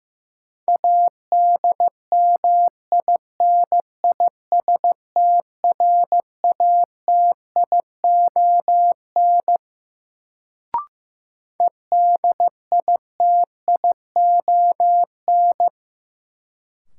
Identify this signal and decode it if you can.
Morse code, heard as a keyed tone, about 15 wpm, 700 Hz. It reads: ADMINISTRATION EDITION
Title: Sets of 5 Words - New Word Drill - 15wpm